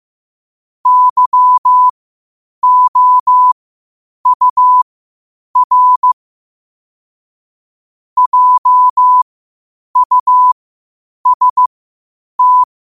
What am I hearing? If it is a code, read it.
Morse code, heard as a keyed tone, about 15 wpm, spaced as for 8 wpm, 1000 Hz. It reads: YOUR JUST
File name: qotdfiles / 8wpm